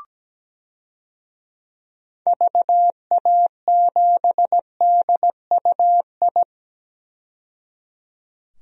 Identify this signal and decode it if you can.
Morse code, heard as a keyed tone, about 17 wpm, 700 Hz. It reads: VA7DUI
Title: DXCC-100 Call Signs - 6 Characters - 17wpm